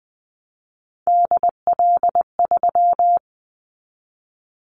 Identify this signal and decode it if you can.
Morse code, heard as a keyed tone, about 20 wpm, 700 Hz. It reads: DL3
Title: Callsign Prefixes for Top 50 DXCC - 20wpm